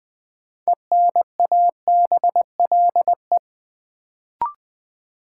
Morse code, heard as a keyed tone, about 20 wpm, 700 Hz. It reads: ENABLE
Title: Sentences from Top 1500 Words - New Word Drill Alt - 20wpm